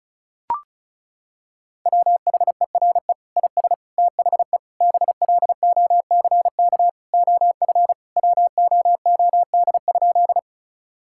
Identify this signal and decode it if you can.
Morse code, heard as a keyed tone, about 36 wpm, 690 Hz. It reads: WHERE IS THE BLOCK OF WOOD?
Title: Sentences from Top 1600 Words 35wpm